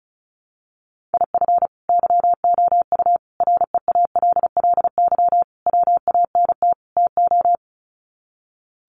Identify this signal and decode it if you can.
Morse code, heard as a keyed tone, about 34 wpm, 700 Hz. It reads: IF YOU REALLY WANT TO
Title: Sets of 5 Words Encore2 35wpm